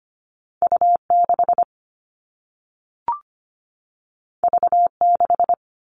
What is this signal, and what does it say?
Morse code, heard as a keyed tone, about 25 wpm, 700 Hz. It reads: U 6 V 6
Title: All Cut Numbers - 2-Digit 25wpm